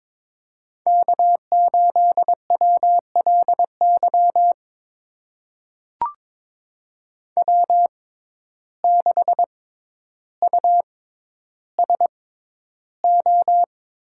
Morse code, heard as a keyed tone, about 22 wpm, spaced as for 11 wpm, 700 Hz. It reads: K8WLY W6USO
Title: USA Call Signs - 5 Characters - 6x Character Spacing - 22wpm